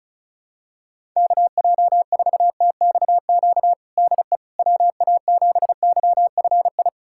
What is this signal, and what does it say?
Morse code, heard as a keyed tone, about 34 wpm, 700 Hz. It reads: KJ4TXQ DE WA7YFI
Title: USA Callsign DE Callsign - 6 Characters - 35wpm